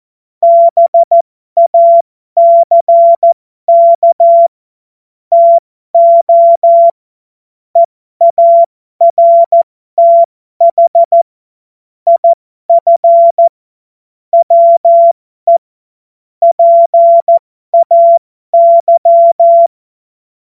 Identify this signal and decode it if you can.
Morse code, heard as a keyed tone, about 14 wpm, spaced as for 12 wpm, 680 Hz. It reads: BACK TO EARTH IF WE PAY